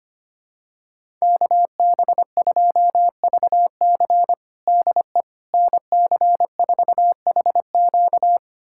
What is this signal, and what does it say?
Morse code, heard as a keyed tone, about 25 wpm, 700 Hz. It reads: KB2VC DE NC4HQ
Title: USA Callsign DE Callsign - 5 Characters - 25wpm